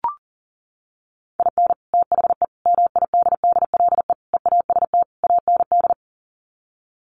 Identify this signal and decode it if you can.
Morse code, heard as a keyed tone, about 40 wpm, 700 Hz. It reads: IN THE MIDDLE EAST AND
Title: Sets of 5 Words Encore 40wpm